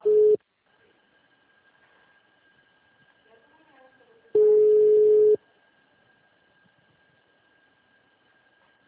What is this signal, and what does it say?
Ringback tone (Europe, 425 Hz), with speech.
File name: out- -2026- - - .7599